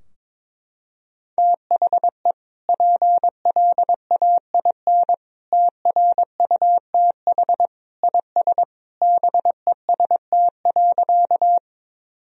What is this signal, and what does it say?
Morse code, heard as a keyed tone, about 22 wpm, 700 Hz. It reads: THE PLAIN TRUTH IS BEST.
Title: Sentences from Top 1000 Words - Review Part 2 - 22wpm